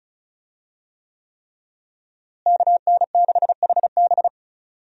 Morse code, heard as a keyed tone, about 35 wpm, 700 Hz. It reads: KN6HB
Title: USA Call Signs - 5 Characters - Repeated 3x - 35wpm